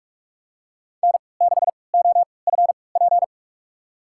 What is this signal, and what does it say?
Morse code, heard as a keyed tone, about 46 wpm, spaced as for 25 wpm, 700 Hz. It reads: N6OFP
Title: USA Call Signs - 5 Characters - 3x Character Spacing - 45wpm